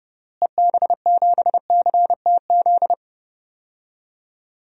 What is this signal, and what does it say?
Morse code, heard as a keyed tone, about 32 wpm, 700 Hz. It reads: EB7CTZ
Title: DXCC-100 Call Signs - 6 Characters - Repeated 3x - 30wpm